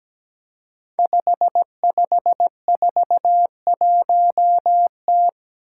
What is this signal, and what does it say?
Morse code, heard as a keyed tone, about 17 wpm, 700 Hz. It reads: 5541T
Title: Five Challenging Letter-Number Combinations 17wpm